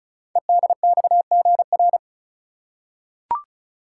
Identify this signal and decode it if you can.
Morse code, heard as a keyed tone, about 36 wpm, 690 Hz. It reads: EDXGR
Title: Code Groups - 5 Letters - 35wpm